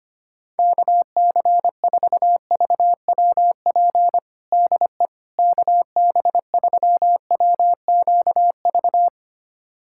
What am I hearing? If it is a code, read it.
Morse code, heard as a keyed tone, about 25 wpm, 700 Hz. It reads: KC4VWP DE KB3WQV
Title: USA Callsign DE Callsign 25wpm